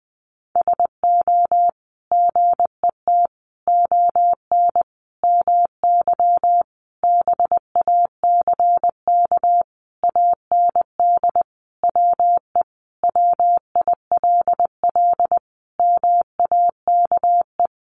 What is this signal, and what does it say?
Morse code, heard as a keyed tone, about 20 wpm, 700 Hz. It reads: SO GET ON MY BACK AND WE WILL MAKE
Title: Sentences from Top 1000 Words 20wpm